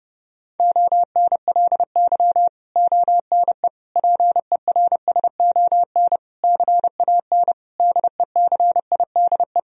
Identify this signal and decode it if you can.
Morse code, heard as a keyed tone, about 30 wpm, 700 Hz. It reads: ONLY ONE PERSON CAN DECIDE
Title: Sentences from Top 1000 Words - Review Part 3 - 30wpm